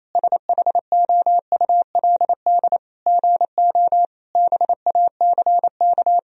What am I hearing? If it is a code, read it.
Morse code, heard as a keyed tone, about 28 wpm, 700 Hz. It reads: SHOULD GO BACK
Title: Sentences from Top 1000 Words - Review Part 4 - 28wpm